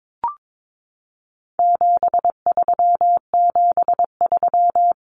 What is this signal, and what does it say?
Morse code, heard as a keyed tone, about 22 wpm, 700 Hz. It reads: 7373
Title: Common Cut Numbers - 4-Digit 22wpm